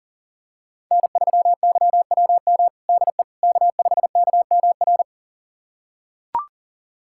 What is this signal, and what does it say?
Morse code, heard as a keyed tone, about 37 wpm, 700 Hz. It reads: N3YWM DE K5KMR
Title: USA Callsign DE Callsign - 5 Characters - 40wpm